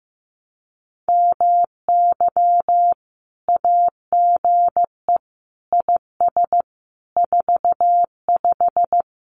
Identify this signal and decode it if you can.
Morse code, heard as a keyed tone, about 15 wpm, 700 Hz. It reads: MY AGE IS 45
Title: Common QSO Phrases Encore 15wpm